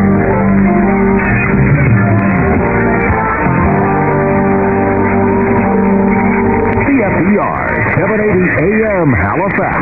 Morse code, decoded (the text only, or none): none